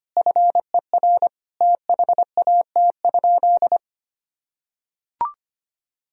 Morse code, heard as a keyed tone, about 25 wpm, 700 Hz. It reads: FER THAT?
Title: Sentences from Top 300 Words and Abbreviations 25wpm